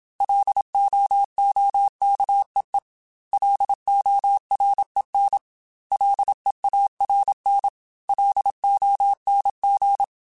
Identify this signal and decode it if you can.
Morse code, heard as a keyed tone, about 27 wpm, 800 Hz. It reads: LOOKEE LOREN LEARN LONG